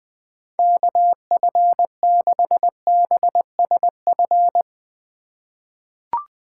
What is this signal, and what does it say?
Morse code, heard as a keyed tone, about 21 wpm, 700 Hz. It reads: KF6BSF
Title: USA Call Signs - 6 Characters - 2x Character Spacing - 20wpm